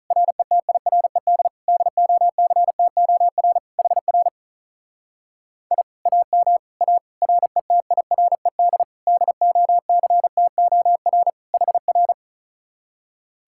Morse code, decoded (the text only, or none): RETIRED DOCTOR HR I AM A RETIRED DOCTOR HR